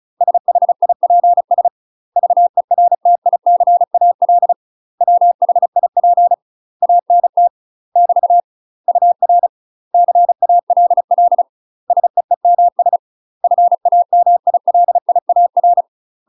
Morse code, heard as a keyed tone, about 36 wpm, 700 Hz. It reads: SHIPS VERTICAL WHIP ANT <BT> UR CALL SEEMS FAMILIAR
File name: UK MASTER QSOs TRAINING / 35WPM / UK MASTER QSOs 0018